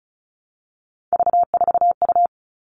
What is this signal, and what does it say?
Morse code, heard as a keyed tone, about 34 wpm, 700 Hz. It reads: V4U
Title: All Cut Numbers - 3-Digit 35wpm